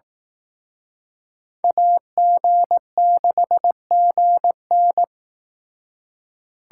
Morse code, heard as a keyed tone, about 18 wpm, 700 Hz. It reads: AG6GN
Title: Speed Racing - USA 2x2 Call Signs 15wpm